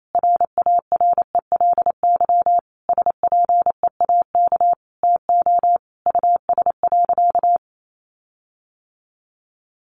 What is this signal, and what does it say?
Morse code, heard as a keyed tone, about 28 wpm, 700 Hz. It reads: RARELY SPEAK TO US.
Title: Sentences from Top 1400 Words 28wpm